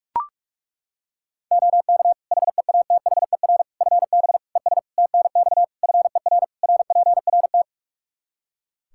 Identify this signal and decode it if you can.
Morse code, heard as a keyed tone, about 41 wpm, 700 Hz. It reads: OK HEATHER FB ES TNX FER RPRT
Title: K1USN SST QSO Practice 45wpm